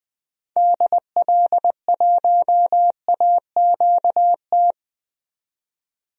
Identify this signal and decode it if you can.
Morse code, heard as a keyed tone, about 20 wpm, 700 Hz. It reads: DL1AQT
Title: DXCC-100 Call Signs - 6 Characters - 20wpm